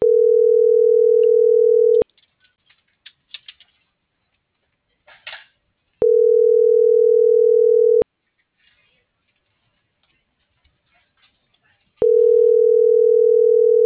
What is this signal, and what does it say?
Ringback tone (North America), 440 + 480 Hz.